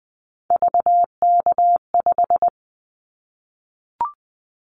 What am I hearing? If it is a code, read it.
Morse code, heard as a keyed tone, about 19 wpm, 700 Hz. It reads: VK5
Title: Callsign Prefixes for Top 100 DXCC - 20wpm